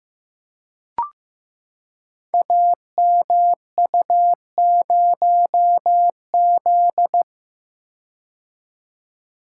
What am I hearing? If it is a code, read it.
Morse code, heard as a keyed tone, about 15 wpm, 700 Hz. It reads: AMU0Z
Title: Code Groups - 5 Letter:Numbers - 15wpm